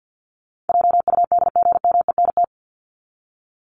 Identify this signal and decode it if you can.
Morse code, heard as a keyed tone, about 50 wpm, 700 Hz. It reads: JUDGMENT
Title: Top 1751-2000 Words 50wpm